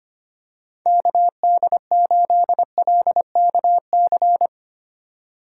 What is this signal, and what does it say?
Morse code, heard as a keyed tone, about 25 wpm, 700 Hz. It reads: KD8LKC
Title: USA Call Signs - 6 Characters - 2x Character Spacing - 25wpm